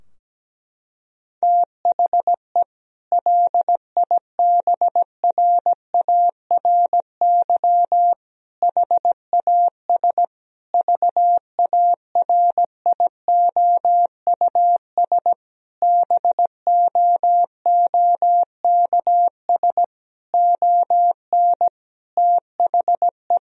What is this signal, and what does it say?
Morse code, heard as a keyed tone, about 17 wpm, 700 Hz. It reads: THE LIBRARY HAS VARIOUS BOOKS ON THE
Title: Sentences from Top 1000 Words - Review Part 4 - 17wpm